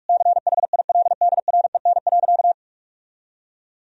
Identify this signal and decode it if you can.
Morse code, heard as a keyed tone, about 46 wpm, 700 Hz. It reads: CHILDREN.
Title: Sentences from Top 400 Words and Abbreviations 45wpm